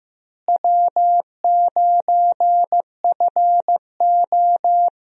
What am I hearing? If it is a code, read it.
Morse code, heard as a keyed tone, about 15 wpm, 700 Hz. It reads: W9FO
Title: USA Call Signs - 4 Characters - Repeated 3x - 15wpm